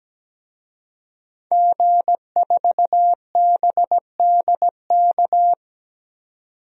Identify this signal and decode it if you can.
Morse code, heard as a keyed tone, about 17 wpm, 700 Hz. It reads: G4BDK
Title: DXCC-100 Call Signs - 5 Characters - 17wpm